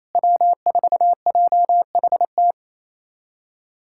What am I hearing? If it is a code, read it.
Morse code, heard as a keyed tone, about 28 wpm, 700 Hz. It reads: W4JHT
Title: USA Callsign DE Callsign 28wpm